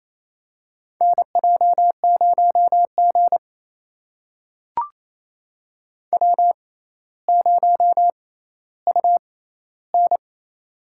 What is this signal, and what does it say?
Morse code, heard as a keyed tone, about 28 wpm, spaced as for 14 wpm, 700 Hz. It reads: NJ0G W0UN
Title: USA Call Signs - 4 Characters - 6x Character Spacing - 28wpm